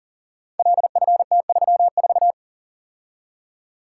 Morse code, heard as a keyed tone, about 38 wpm, 700 Hz. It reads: LFT34
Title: Code Groups - 5 Letter:Numbers - 40wpm